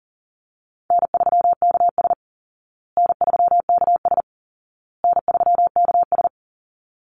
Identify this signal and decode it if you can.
Morse code, heard as a keyed tone, about 40 wpm, 700 Hz. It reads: N3KS N3KS N3KS
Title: Easy Call Signs Repeated 3x - 40wpm